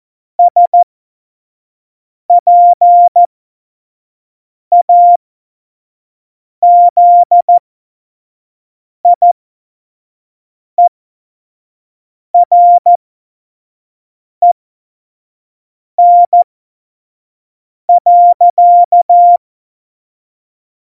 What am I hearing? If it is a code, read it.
Morse code, heard as a keyed tone, about 14 wpm, spaced as for 5 wpm, 700 Hz. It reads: SPAZIEREN.
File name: Morsekurs / Witze / Witz 13. November 2018, 25 F70